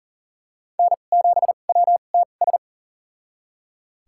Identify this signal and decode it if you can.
Morse code, heard as a keyed tone, about 40 wpm, spaced as for 29 wpm, 700 Hz. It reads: N7WTS
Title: USA Call Signs - 2x Character Spacing - 40wpm